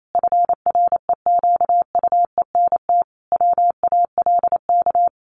Morse code, heard as a keyed tone, about 28 wpm, 700 Hz. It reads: FREQUENT WALK
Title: Sentences from Top 1400 Words 28wpm